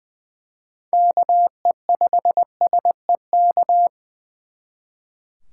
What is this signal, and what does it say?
Morse code, heard as a keyed tone, about 24 wpm, 700 Hz. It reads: KE5SEK
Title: Speed Racing - USA 2x3 Call Signs 20wpm